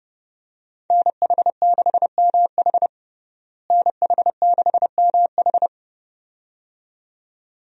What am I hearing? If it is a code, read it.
Morse code, heard as a keyed tone, about 30 wpm, 700 Hz. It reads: NH6MH NH6MH
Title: USA Call Signs - 5 Characters - Repeated 2x - 30wpm